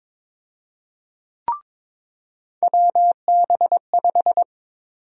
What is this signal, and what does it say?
Morse code, heard as a keyed tone, about 22 wpm, 700 Hz. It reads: WB5
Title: USA Call Sign Prefixes - 22wpm